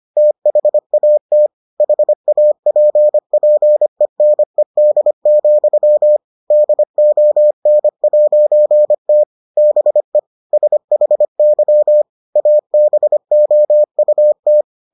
Morse code, heard as a keyed tone, about 25 wpm, 600 Hz. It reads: THAT HAPPENED, DON'T BE SHY ABOUT